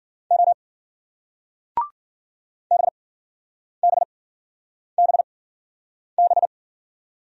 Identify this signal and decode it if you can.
Morse code, heard as a keyed tone, about 56 wpm, spaced as for 9 wpm, 700 Hz. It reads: O BBBB